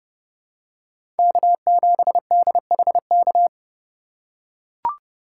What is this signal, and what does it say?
Morse code, heard as a keyed tone, about 30 wpm, 700 Hz. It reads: K7DHK
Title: USA Call Signs - 5 Characters - 3x Character Spacing - 30wpm